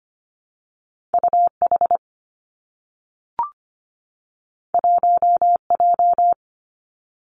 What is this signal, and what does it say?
Morse code, heard as a keyed tone, about 25 wpm, 700 Hz. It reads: U H 1 J
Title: Two Challenging Letter-Number Combinations 25wpm